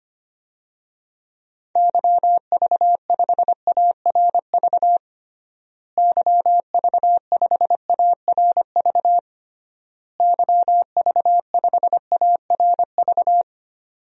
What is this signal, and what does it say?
Morse code, heard as a keyed tone, about 25 wpm, 700 Hz. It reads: YV5ARV YV5ARV YV5ARV
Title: DXCC-100 Call Signs - 6 Characters - Repeated 3x - 25wpm